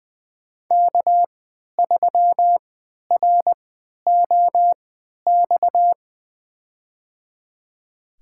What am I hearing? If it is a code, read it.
Morse code, heard as a keyed tone, about 20 wpm, spaced as for 11 wpm, 700 Hz. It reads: K3ROX
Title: USA Call Signs - 3x Character Spacing - 20wpm